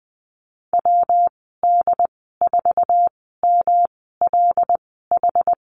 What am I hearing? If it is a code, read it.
Morse code, heard as a keyed tone, about 20 wpm, spaced as for 15 wpm, 700 Hz. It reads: WD4MLH
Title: 193 Lesson 31g - Letter v - Call Signs 20wpm